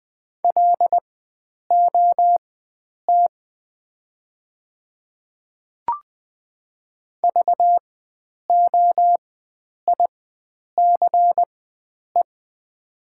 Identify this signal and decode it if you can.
Morse code, heard as a keyed tone, about 20 wpm, spaced as for 9 wpm, 700 Hz. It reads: LOT VOICE
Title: 4x Character Spacing - Top 500 Words - 20wpm